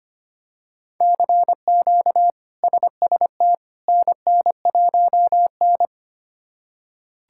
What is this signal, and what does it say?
Morse code, heard as a keyed tone, about 25 wpm, 700 Hz. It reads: CQ SST NN1N
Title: K1USN SST QSO Practice - 25wpm